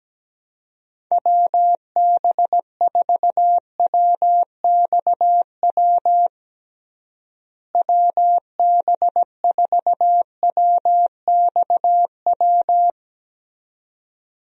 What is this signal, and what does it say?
Morse code, heard as a keyed tone, about 17 wpm, 700 Hz. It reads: WB4WXW WB4WXW
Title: USA Call Signs - 6 Characters - Repeated 2x - 17wpm